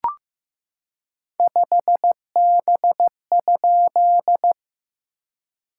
Morse code, heard as a keyed tone, about 15 wpm, 690 Hz. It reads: 5B?
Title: SOTA QSO Practice 15wpm